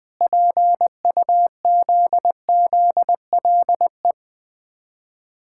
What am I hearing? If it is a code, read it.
Morse code, heard as a keyed tone, about 20 wpm, 700 Hz. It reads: PUZZLE
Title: Difficult Words with Hints - 20wpm